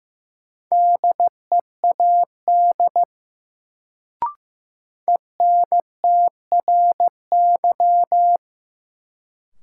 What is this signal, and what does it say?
Morse code, heard as a keyed tone, about 15 wpm, 700 Hz. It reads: DEAD ENTRY
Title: Sentences from Top 1500 Words - New Word Drill Alt - 15wpm